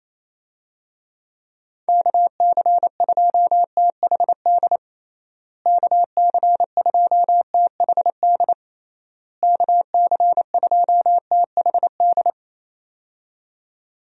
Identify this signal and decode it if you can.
Morse code, heard as a keyed tone, about 28 wpm, 700 Hz. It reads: KC2THD KC2THD KC2THD
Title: USA Call Signs - 6 Characters - Repeated 3x - 28wpm